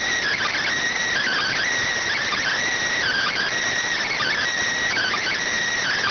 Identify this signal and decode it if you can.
Emergency Alert System (SAME) tones (header bursts), heard as harsh buzzing bursts. It is heard from 1.1 to 5.4 s.